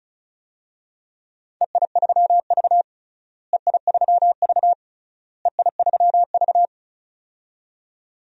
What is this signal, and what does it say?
Morse code, heard as a keyed tone, about 36 wpm, 700 Hz. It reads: EI3V EI3V EI3V
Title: DXCC-100 Call Signs - 4 Characters - Repeated 3x - 35wpm